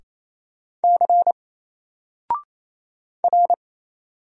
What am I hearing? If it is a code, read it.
Morse code, heard as a keyed tone, about 27 wpm, 700 Hz. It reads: C R
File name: Single-Letter-Number-28wpm